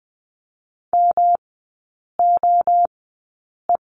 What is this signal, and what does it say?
Morse code, heard as a keyed tone, about 20 wpm, spaced as for 8 wpm, 700 Hz. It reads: MOE
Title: 107 Lesson 18e - Letter m - Three Characters 20wpm